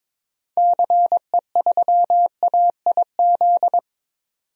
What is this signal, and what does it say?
Morse code, heard as a keyed tone, about 22 wpm, 700 Hz. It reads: CE3AIZ
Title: DXCC-100 Call Signs - 6 Characters - 22wpm